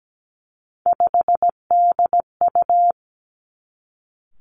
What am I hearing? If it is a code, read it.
Morse code, heard as a keyed tone, about 17 wpm, 700 Hz. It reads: 5DU
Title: Three Challenging Letter-Number Combinations 17wpm